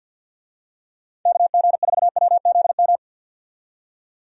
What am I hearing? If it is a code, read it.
Morse code, heard as a keyed tone, about 46 wpm, 700 Hz. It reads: KG4WZM
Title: USA Call Signs - 6 Characters - 50wpm